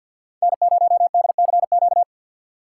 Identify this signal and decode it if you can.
Morse code, heard as a keyed tone, about 51 wpm, 700 Hz. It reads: N0DCQ
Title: USA Call Signs - 3x Character Spacing - 50wpm